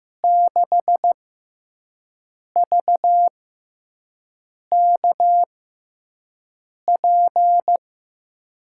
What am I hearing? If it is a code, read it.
Morse code, heard as a keyed tone, about 15 wpm, spaced as for 5 wpm, 700 Hz. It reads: 6VKP
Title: USA Call Signs - 6 Characters - 6x Character Spacing - 15wpm